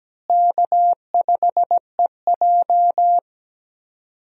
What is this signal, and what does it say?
Morse code, heard as a keyed tone, about 17 wpm, 700 Hz. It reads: K5EJ
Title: USA Call Signs - 4 Characters - Repeated 3x - 17wpm